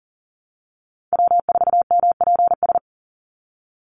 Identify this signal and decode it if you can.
Morse code, heard as a keyed tone, about 39 wpm, 700 Hz. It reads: W4MPS
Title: NA SOTA Chaser Callsigns 40wpm